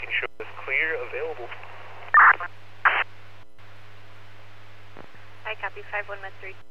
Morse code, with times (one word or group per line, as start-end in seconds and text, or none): none